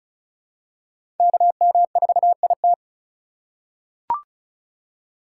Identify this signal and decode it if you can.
Morse code, heard as a keyed tone, about 34 wpm, 700 Hz. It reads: KM4IT
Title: USA Call Signs - 4x Character Spacing - 35wpm